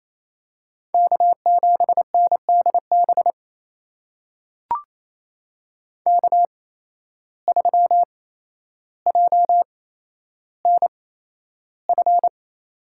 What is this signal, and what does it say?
Morse code, heard as a keyed tone, about 27 wpm, spaced as for 12 wpm, 700 Hz. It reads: K7NDB K3JNF
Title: USA Call Signs - 5 Characters - 8x Character Spacing - 28wpm